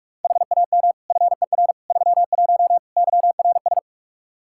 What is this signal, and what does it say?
Morse code, heard as a keyed tone, about 46 wpm, 700 Hz. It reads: HAM FER 31 YRS
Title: K1USN SST QSO Practice 45wpm